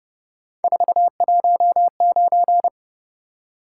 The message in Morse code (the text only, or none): 419